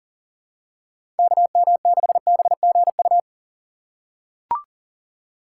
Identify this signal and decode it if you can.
Morse code, heard as a keyed tone, about 38 wpm, 700 Hz. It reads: KM6BGU
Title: USA Call Signs - 6 Characters - 3x Character Spacing - 40wpm